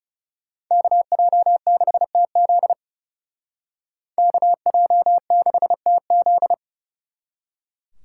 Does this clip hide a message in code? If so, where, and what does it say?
Yes: Morse code from 0.7 to 6.5 s, heard as a keyed tone, about 39 wpm, 700 Hz. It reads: KJ6TZ KJ6TZ